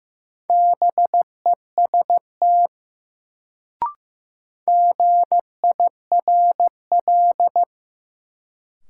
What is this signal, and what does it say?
Morse code, heard as a keyed tone, about 16 wpm, 700 Hz. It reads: BEST GIRL
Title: Sentences from Top 400 Words - New Word Drill - 15wpm